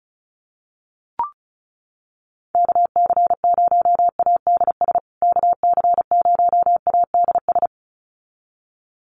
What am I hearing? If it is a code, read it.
Morse code, heard as a keyed tone, about 34 wpm, 700 Hz. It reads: KC0ADS KC0ADS
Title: USA Call Signs Repeated 2x 35wpm